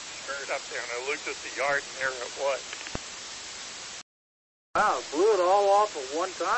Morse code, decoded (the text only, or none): none